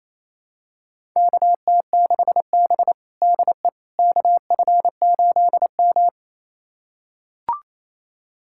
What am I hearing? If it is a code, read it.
Morse code, heard as a keyed tone, about 28 wpm, 700 Hz. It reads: KT6B DE KF8M
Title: USA Callsign DE Callsign - 4 Characters - 28wpm